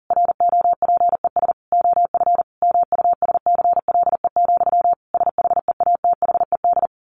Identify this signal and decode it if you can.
Morse code, heard as a keyed tone, about 40 wpm, 700 Hz. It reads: ROPES OF MUSCLE, SHEATHED
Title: A Sound Of Thunder 40wpm